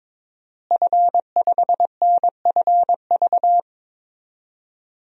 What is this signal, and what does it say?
Morse code, heard as a keyed tone, about 22 wpm, 700 Hz. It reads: F5NFV
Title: DXCC-100 Call Signs - 5 Characters - Repeated 3x - 22wpm